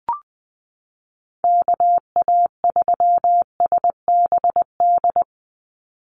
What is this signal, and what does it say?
Morse code, heard as a keyed tone, about 19 wpm, 700 Hz. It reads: KA3SBD
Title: USA Call Signs 20wpm